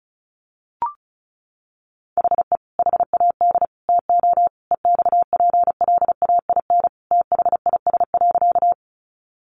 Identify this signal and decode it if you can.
Morse code, heard as a keyed tone, about 34 wpm, 700 Hz. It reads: HE HAD TO EXPLAIN THIS.
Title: Sentences from Top 1100 Words 35wpm